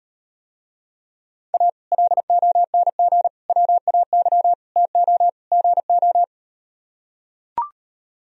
Morse code, heard as a keyed tone, about 38 wpm, 700 Hz. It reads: A LONG WAY TO GO